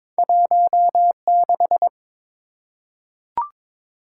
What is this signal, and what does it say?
Morse code, heard as a keyed tone, about 22 wpm, 700 Hz. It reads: 16
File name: practice / 22wpm / Two Letter-Number Combinations - 22wpm